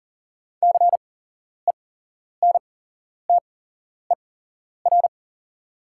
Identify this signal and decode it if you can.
Morse code, heard as a keyed tone, about 39 wpm, spaced as for 11 wpm, 700 Hz. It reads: CENTER
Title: 8x Character Spacing - Top 500 Words - 40wpm